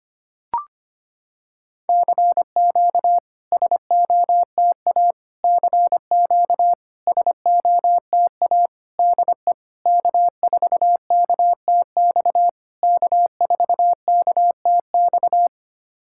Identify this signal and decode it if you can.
Morse code, heard as a keyed tone, about 25 wpm, 700 Hz. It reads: CQ SOTA CQ SOTA DE K4KTX K4KTX
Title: SOTA QSO Practice 25wpm